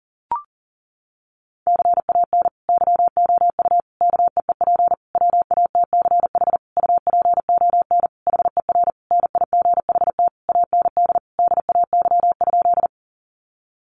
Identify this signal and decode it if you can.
Morse code, heard as a keyed tone, about 40 wpm, 700 Hz. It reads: CAN YOU KEEP WATCH UPON HER NIGHT AND DAY?
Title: Sentences from Top 1100 Words 40wpm